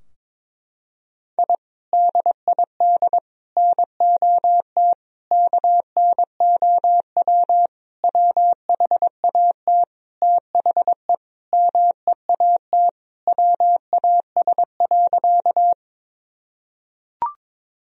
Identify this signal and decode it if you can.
Morse code, heard as a keyed tone, about 22 wpm, 700 Hz. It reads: I DID NOT KNOW WHAT THE MEAT WAS.
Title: Sentences from Top 1000 Words - Review Part 3 - 22wpm